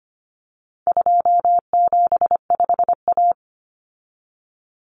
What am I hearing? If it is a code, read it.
Morse code, heard as a keyed tone, about 25 wpm, 700 Hz. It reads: 275A
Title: Common Cut Numbers - 4-Digit 25wpm